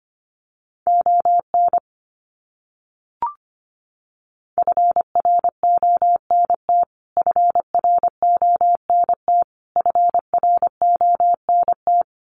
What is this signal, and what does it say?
Morse code, heard as a keyed tone, about 25 wpm, 700 Hz. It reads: ON FRONT FRONT FRONT
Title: Top 500 Words Repeated 25wpm